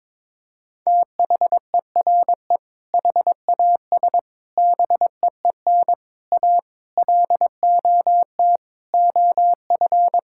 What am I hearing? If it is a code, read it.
Morse code, heard as a keyed tone, about 22 wpm, 700 Hz. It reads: THERE HAS BEEN A LOT OF
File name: practice / 22wpm / Sets of 7 Words - 22wpm